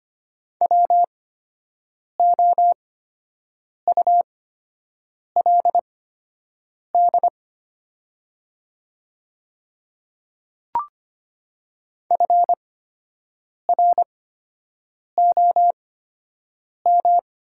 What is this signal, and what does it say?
Morse code, heard as a keyed tone, about 25 wpm, spaced as for 7 wpm, 700 Hz. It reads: WOULD FROM
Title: 8x Character Spacing - Top 100 Words - 25wpm